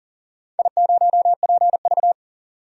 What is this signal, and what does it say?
Morse code, heard as a keyed tone, about 40 wpm, 700 Hz. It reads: I0PV
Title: DXCC-100 Call Signs - 4 Characters - 40wpm